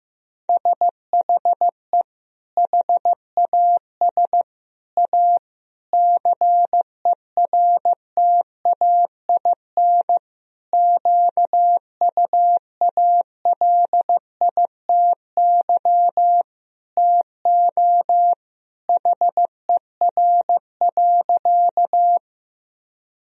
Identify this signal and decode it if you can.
Morse code, heard as a keyed tone, about 16 wpm, 700 Hz. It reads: SHE HAS A CERTAIN QUALITY TO HER.
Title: Sentences from Top 1000 Words - Review Part 4 - 15wpm